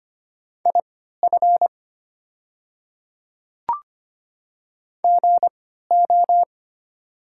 Morse code, heard as a keyed tone, about 25 wpm, spaced as for 14 wpm, 700 Hz. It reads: IF GO
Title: 3x Character Spacing - Top 100 Words - 25wpm